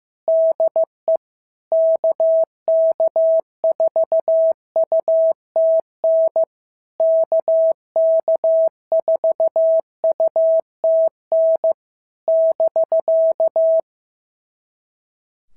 Morse code, heard as a keyed tone, about 14 wpm, 650 Hz. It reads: DE KK4UTN KK4UTN <BK>
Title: SOTA QSO Practice 15wpm